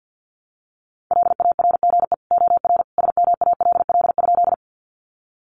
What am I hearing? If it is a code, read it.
Morse code, heard as a keyed tone, about 48 wpm, 700 Hz. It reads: LARGE OR SMALL?
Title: Sentences from Top 200 Words 50wpm